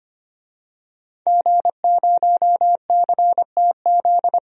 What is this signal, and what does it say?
Morse code, heard as a keyed tone, about 25 wpm, 700 Hz. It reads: G0CTZ
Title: DXCC-100 Call Signs - 5 Characters - 25wpm